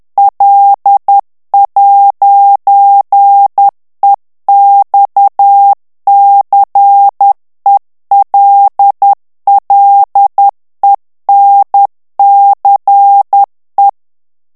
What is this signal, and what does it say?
Morse code, heard as a keyed tone, about 10 wpm, 800 Hz. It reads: L'EXCELLENCE
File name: morse mp3 / FRANCAIS TEXTE CLAIR 2 VITESSE 10 MOTS MINUTES